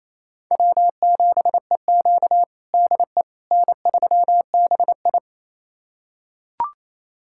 Morse code, heard as a keyed tone, about 28 wpm, 700 Hz. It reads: W7EQ DE N3BI
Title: USA Callsign DE Callsign - 4 Characters - 28wpm